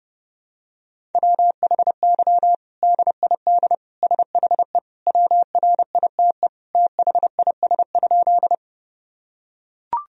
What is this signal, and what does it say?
Morse code, heard as a keyed tone, about 30 wpm, 700 Hz. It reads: WHY DID SHE WRITE THIS?
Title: Sentences from Top 1000 Words - Review Part 1 - 30wpm